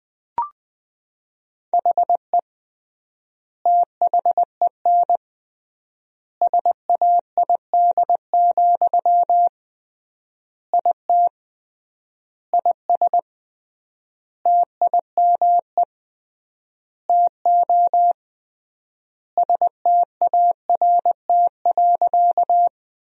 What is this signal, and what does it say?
Morse code, heard as a keyed tone, about 20 wpm, 700 Hz. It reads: HE THEN SAID, IT IS TIME TO START.
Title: 3x Word-Spacing - Sentences from Top 300 Words - 20wpm